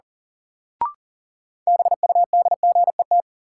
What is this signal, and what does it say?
Morse code, heard as a keyed tone, about 37 wpm, 700 Hz. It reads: BUDGET
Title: Sentences from Top 1200 Words - New Word Drill Alt - 40wpm